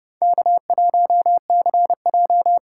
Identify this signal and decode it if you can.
Morse code, heard as a keyed tone, about 28 wpm, 700 Hz. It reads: K1CJ